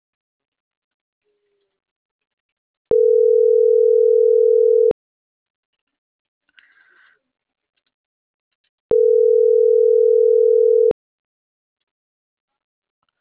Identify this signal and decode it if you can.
Ringback tone (North America), 440 + 480 Hz.